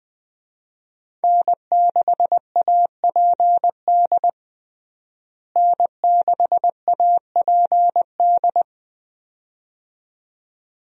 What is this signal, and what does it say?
Morse code, heard as a keyed tone, about 20 wpm, 700 Hz. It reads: N6APD N6APD
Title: USA Call Signs - 5 Characters - Repeated 2x - 20wpm